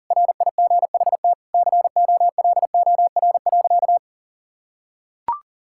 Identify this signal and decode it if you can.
Morse code, heard as a keyed tone, about 40 wpm, 700 Hz. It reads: RIGHT COLOR.
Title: Sentences from Top 1000 Words - Review Part 6 - 40wpm